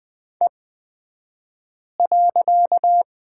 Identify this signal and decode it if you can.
Morse code, heard as a keyed tone, about 20 wpm, spaced as for 5 wpm, 700 Hz. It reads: E.
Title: 254 Lesson 41d - Period - Two Characters 20wpm